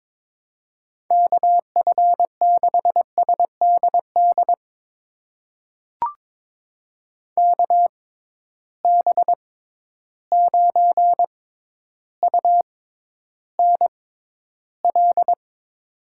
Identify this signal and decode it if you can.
Morse code, heard as a keyed tone, about 22 wpm, spaced as for 11 wpm, 700 Hz. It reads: KF6SDD KB9UNL
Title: USA Call Signs - 6 Characters - 6x Character Spacing - 22wpm